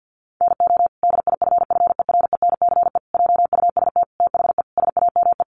Morse code, heard as a keyed tone, about 50 wpm, 700 Hz. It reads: NO DIFFERENCE JUST THE SAME
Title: Sentences from Top 800 Words 50wpm